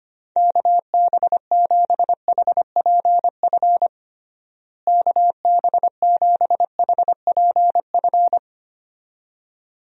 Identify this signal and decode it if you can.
Morse code, heard as a keyed tone, about 25 wpm, 700 Hz. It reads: KB7HPF KB7HPF
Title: USA Call Signs - 6 Characters - Repeated 2x - 25wpm